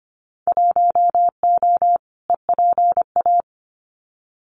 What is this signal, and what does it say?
Morse code, heard as a keyed tone, about 25 wpm, 700 Hz. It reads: 1O EPA
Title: Winter Field Day Exchanges 25wpm